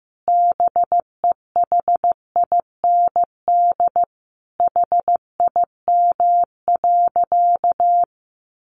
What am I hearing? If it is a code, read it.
Morse code, heard as a keyed tone, about 15 wpm, 700 Hz. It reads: BEHIND HIM.
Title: Sentences from Top 1400 Words 15wpm